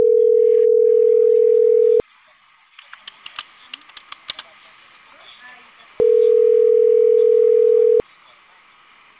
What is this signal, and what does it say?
Ringback tone (North America), 440 + 480 Hz, with speech.
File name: CENTRAL CAEA / out agcaea02 - .17450